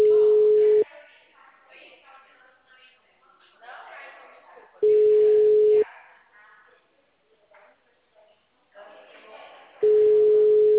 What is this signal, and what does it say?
Ringback tone (Europe, 425 Hz), with speech.